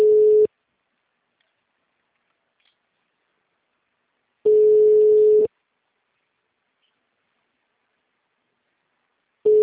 Ringback tone (Europe, 425 Hz).
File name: out- -2002- - - .15236